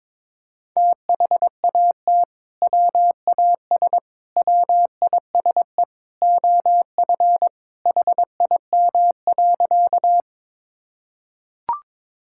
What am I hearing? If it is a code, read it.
Morse code, heard as a keyed tone, about 22 wpm, 700 Hz. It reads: THAT WAS WISE OF HIM.
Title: Sentences from Top 1600 Words 22wpm